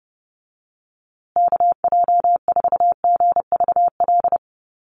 Morse code, heard as a keyed tone, about 30 wpm, 700 Hz. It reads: KJ4GVL
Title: USA Call Signs Repeated 3x 30wpm